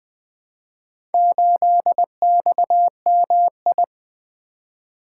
Morse code, heard as a keyed tone, about 20 wpm, 700 Hz. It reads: 8XMI
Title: Four Letter-Number Combinations - 20wpm